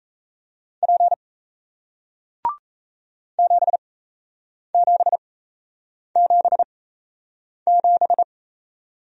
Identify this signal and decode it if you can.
Morse code, heard as a keyed tone, about 37 wpm, spaced as for 8 wpm, 700 Hz. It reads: P 7777